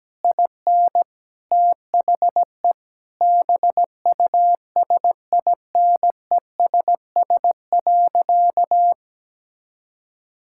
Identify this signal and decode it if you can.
Morse code, heard as a keyed tone, about 17 wpm, 700 Hz. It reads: IN THE BUSINESS.
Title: Sentences from Top 1000 Words - Review Part 6 - 17wpm